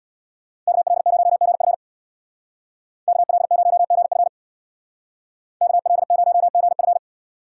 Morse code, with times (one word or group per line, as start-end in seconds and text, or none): 0.7-1.7 KL0GF
3.1-4.3 KL0GF
5.6-7.0 KL0GF